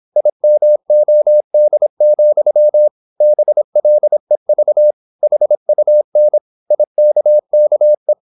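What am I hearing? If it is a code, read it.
Morse code, heard as a keyed tone, about 26 wpm, 600 Hz. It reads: IMOD, BLEV HUN IKKE
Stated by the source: High-speed (QRQ) Morse at 26 wpm.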